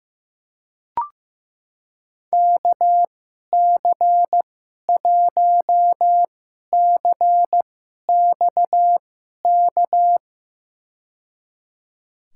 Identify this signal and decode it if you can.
Morse code, heard as a keyed tone, about 15 wpm, spaced as for 11 wpm, 700 Hz. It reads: KC1CXK
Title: USA Call Signs - 6 Characters - 2x Character Spacing - 15wpm